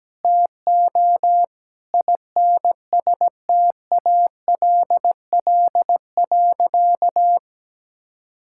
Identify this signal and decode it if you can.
Morse code, heard as a keyed tone, about 17 wpm, 700 Hz. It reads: TO INSTALL.
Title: Sentences from Top 1600 Words 17wpm